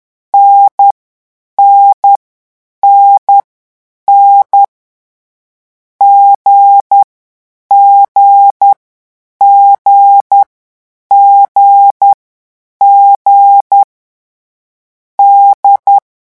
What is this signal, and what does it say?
Morse code, heard as a keyed tone, about 10 wpm, 800 Hz. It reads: N N N N G G G G G D
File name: morse mp3 / lecon4-vitesse 10